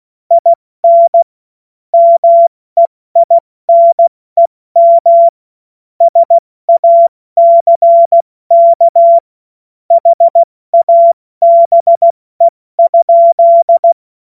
Morse code, heard as a keyed tone, about 16 wpm, 680 Hz. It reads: IN MEINEM SACK HABE?
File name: Morsekurs / Witze / Witz 25. April 70 F80